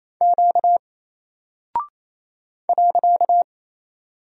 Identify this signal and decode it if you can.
Morse code, heard as a keyed tone, about 28 wpm, 700 Hz. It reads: Q .
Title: Single Character (Letter:Number:Punctuation:Prosign) 28wpm